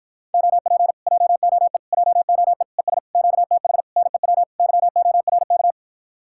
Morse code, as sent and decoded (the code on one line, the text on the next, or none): --- .--. / .--- --- . / .--- --- . / . ... / --.- - .... / -. .-- / -.-- --- .-. -.-
OP JOE JOE ES QTH NEW YORK